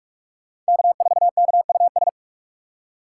Morse code, heard as a keyed tone, about 56 wpm, 700 Hz. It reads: K4KUS
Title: Speed Racing - USA 1x3 Call Signs 45wpm